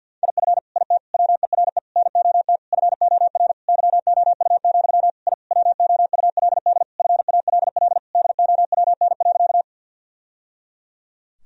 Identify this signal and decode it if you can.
Morse code, heard as a keyed tone, about 50 wpm, 700 Hz. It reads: IF IT WERE NOT FOR YOU, I WOULD FALL DOWN.
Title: Sentences from Top 1000 Words - Review Part 2 - 50wpm